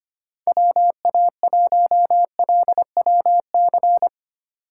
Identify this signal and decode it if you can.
Morse code, heard as a keyed tone, about 25 wpm, 700 Hz. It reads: WA1LWC
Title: K1USN SST QSO Practice - 25wpm